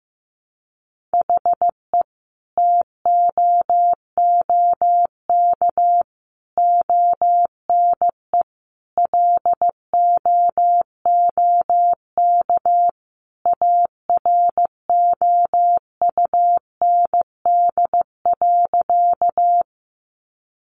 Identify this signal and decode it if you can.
Morse code, heard as a keyed tone, about 15 wpm, 700 Hz. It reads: HE TOOK ONE LOOK AROUND.
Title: Sentences from Top 600 Words 15wpm